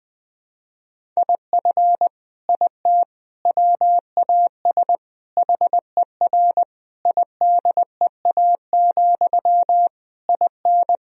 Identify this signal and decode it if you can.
Morse code, heard as a keyed tone, about 20 wpm, 700 Hz. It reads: IF IT WAS HER IDEA, IN
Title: Sentences from Top 1000 Words - Review Part 2 - 20wpm